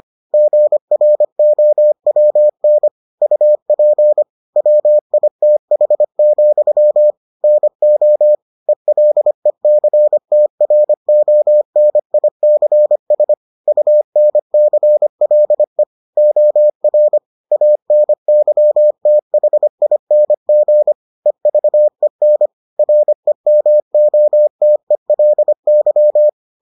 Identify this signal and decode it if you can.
Morse code, heard as a keyed tone, about 25 wpm, 600 Hz. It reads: GROWN UP WITH, NO ELECTRONICS UNCLE OR ANYTHING EVEN REMOTELY